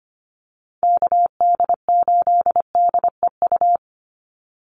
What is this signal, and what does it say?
Morse code, heard as a keyed tone, about 25 wpm, 700 Hz. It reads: KD8DEU
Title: NA SOTA Chaser Callsigns 25wpm